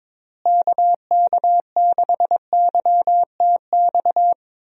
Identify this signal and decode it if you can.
Morse code, heard as a keyed tone, about 21 wpm, 700 Hz. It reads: KK6YTX
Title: USA Call Signs - 6 Characters - Repeated 2x - 22wpm